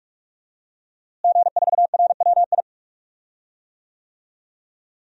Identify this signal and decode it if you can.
Morse code, heard as a keyed tone, about 46 wpm, 700 Hz. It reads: G4RWI
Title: DXCC-100 Call Signs - 5 Characters - Repeated 2x - 45wpm